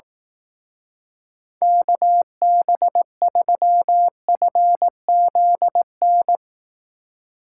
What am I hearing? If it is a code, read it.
Morse code, heard as a keyed tone, about 18 wpm, 700 Hz. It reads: KB3FZN
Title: Speed Racing - USA 2x3 Call Signs 15wpm